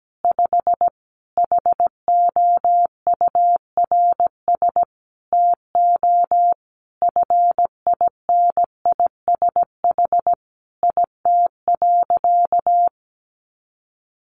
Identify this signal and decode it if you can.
Morse code, heard as a keyed tone, about 17 wpm, 700 Hz. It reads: 5 HOURS TO FINISH IT.
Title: Additional Easy Phrases 17wpm